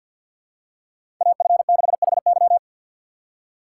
Morse code, heard as a keyed tone, about 47 wpm, 700 Hz. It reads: AF6HY
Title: USA Call Signs - 5 Characters - 2x Character Spacing - 50wpm